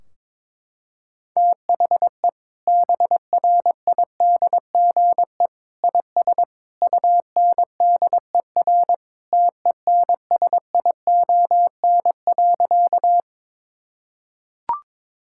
Morse code, heard as a keyed tone, about 22 wpm, 700 Hz. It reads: THE BRIDGE IS UNDER TENSION.
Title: Sentences from Top 1600 Words 22wpm